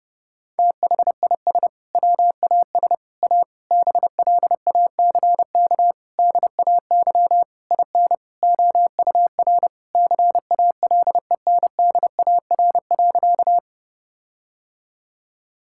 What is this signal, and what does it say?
Morse code, heard as a keyed tone, about 30 wpm, 700 Hz. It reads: THIS WAS A BLACK DAY IN OUR CALENDAR.
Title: Sentences from Top 1600 Words 30wpm